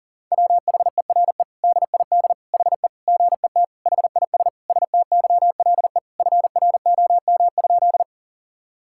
Morse code, heard as a keyed tone, about 40 wpm, 700 Hz. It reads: WHERE DID HE GET HIS STYLE FROM?
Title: Sentences from Top 1000 Words - Review Part 5 - 40wpm